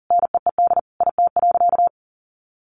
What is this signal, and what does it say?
Morse code, heard as a keyed tone, about 40 wpm, 700 Hz. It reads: NEED IT.
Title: Sentences from Top 1400 Words 40wpm